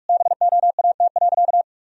Morse code, heard as a keyed tone, about 41 wpm, 700 Hz. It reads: BOAT.
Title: Sentences from Top 1000 Words - Review Part 3 - 45wpm